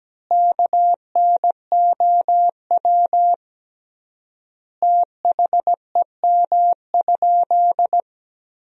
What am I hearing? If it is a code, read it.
Morse code, heard as a keyed tone, about 17 wpm, 700 Hz. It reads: KNOW THEM?
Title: 3x Word-Spacing - Sentences from Top 100 Words - 17wpm